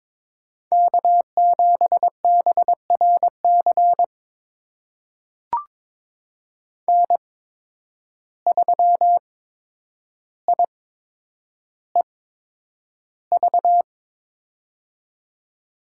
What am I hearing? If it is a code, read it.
Morse code, heard as a keyed tone, about 22 wpm, spaced as for 9 wpm, 700 Hz. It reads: K7BRC N3IEV
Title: USA Call Signs - 8x Character Spacing - 22wpm